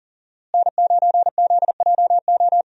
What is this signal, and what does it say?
Morse code, heard as a keyed tone, about 40 wpm, 700 Hz. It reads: N9ZJO
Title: USA Call Signs - 5 Characters - Repeated 3x - 40wpm